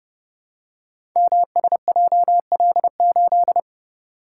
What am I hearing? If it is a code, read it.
Morse code, heard as a keyed tone, about 30 wpm, 700 Hz. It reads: MSJL8
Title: Code Groups - 5 Letter:Numbers - 30wpm